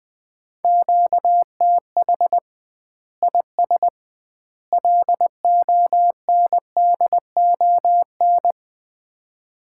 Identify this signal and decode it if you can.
Morse code, heard as a keyed tone, about 20 wpm, 700 Hz. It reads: QTH IS LONDON
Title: Common QSO Phrases - 2x Word-Spacing - 20wpm